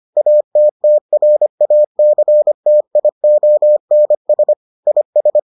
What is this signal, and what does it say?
Morse code, heard as a keyed tone, about 25 wpm, 600 Hz. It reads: ATTRACTIONS IS